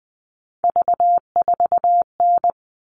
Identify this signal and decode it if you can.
Morse code, heard as a keyed tone, about 20 wpm, 700 Hz. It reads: V4N
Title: All Cut Numbers - 3-Digit 20wpm